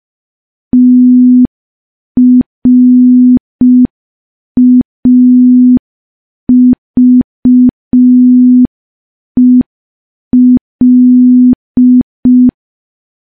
Morse code, heard as a keyed tone, about 5 wpm, 250 Hz. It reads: TRAVEL
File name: morse / morse-training-05WPM